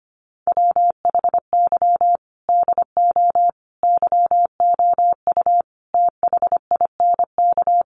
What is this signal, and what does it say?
Morse code, heard as a keyed tone, about 25 wpm, 700 Hz. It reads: WHY DO YOU THINK
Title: Sets of 5 Words Encore2 25wpm